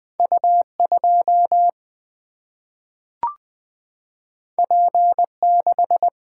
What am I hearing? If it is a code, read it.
Morse code, heard as a keyed tone, about 20 wpm, 700 Hz. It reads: U 2 P 6
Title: Single-Letter Single-Number 20wpm